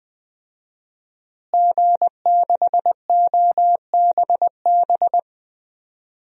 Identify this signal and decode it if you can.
Morse code, heard as a keyed tone, about 20 wpm, 700 Hz. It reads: G6OBB
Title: DXCC-100 Call Signs - 5 Characters - Repeated 2x - 20wpm